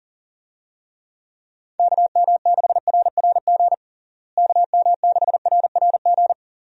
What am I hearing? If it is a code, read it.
Morse code, heard as a keyed tone, about 37 wpm, 700 Hz. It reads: KM6RRG KM6RRG
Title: USA Call Signs - 6 Characters - Repeated 2x - 40wpm